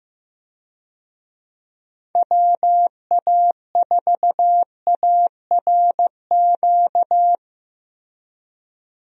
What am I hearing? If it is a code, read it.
Morse code, heard as a keyed tone, about 15 wpm, 700 Hz. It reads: WA4ARQ
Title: USA Call Signs - 6 Characters - Repeated 3x - 15wpm